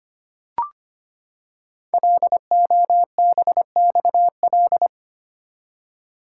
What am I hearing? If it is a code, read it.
Morse code, heard as a keyed tone, about 25 wpm, 700 Hz. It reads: LOBXL
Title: Code Groups - 5 Letters - 25wpm